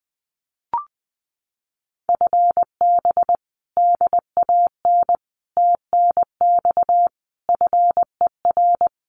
Morse code, heard as a keyed tone, about 19 wpm, 700 Hz. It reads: FB DAN TNX FER
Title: Common QSO Phrases Encore 20wpm